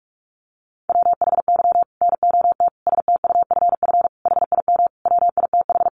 Morse code, heard as a keyed tone, about 44 wpm, 700 Hz. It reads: WHY NOT STUFF HIM WITH
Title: Sentences from Top 1400 Words 45wpm